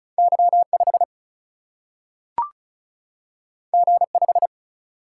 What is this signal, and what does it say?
Morse code, heard as a keyed tone, about 36 wpm, 700 Hz. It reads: Y5 G5
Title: Single-Letter Single-Number 35wpm